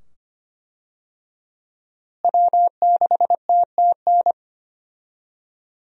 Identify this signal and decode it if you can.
Morse code, heard as a keyed tone, about 25 wpm, 700 Hz. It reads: W6TTN
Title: USA Call Signs - 5 Characters - Repeated 2x - 25wpm